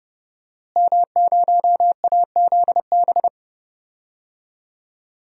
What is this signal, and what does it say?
Morse code, heard as a keyed tone, about 30 wpm, 700 Hz. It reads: M0AZB
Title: DXCC-100 Call Signs - 5 Characters - Repeated 2x - 30wpm